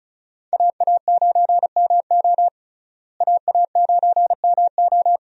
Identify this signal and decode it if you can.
Morse code, heard as a keyed tone, about 34 wpm, 700 Hz. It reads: AA9MO AA9MO
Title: USA Call Signs - 5 Characters - Repeated 2x - 35wpm